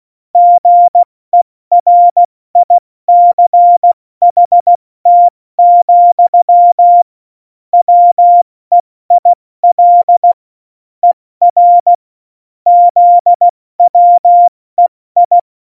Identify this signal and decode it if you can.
Morse code, heard as a keyed tone, about 16 wpm, spaced as for 14 wpm, 700 Hz. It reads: GERICHT, WEIL ER ZWEI
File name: Morsekurs / Witze / Witz 05. Januar 2021, 70 F80